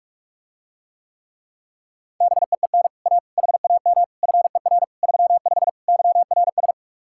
Morse code, heard as a keyed tone, about 49 wpm, 700 Hz. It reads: BEEN A HAM FER 35 YRS